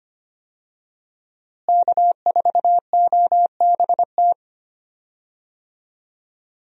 Morse code, heard as a keyed tone, about 25 wpm, 700 Hz. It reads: K4OBT
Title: USA Call Signs - 5 Characters - Repeated 3x - 25wpm